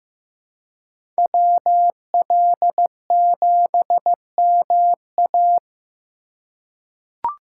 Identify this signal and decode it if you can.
Morse code, heard as a keyed tone, about 15 wpm, 700 Hz. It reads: WL7MA
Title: USA Call Signs - 5 Characters - 4x Character Spacing - 15wpm